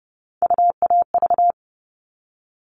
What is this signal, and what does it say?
Morse code, heard as a keyed tone, about 30 wpm, 700 Hz. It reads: UAV
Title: All Cut Numbers - 3-Digit 30wpm